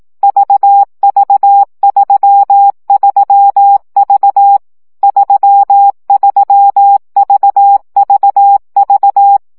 Morse code, heard as a keyed tone, about 18 wpm, 800 Hz. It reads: VV33V 33VVV
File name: CW gyakorlas v 3 15 baud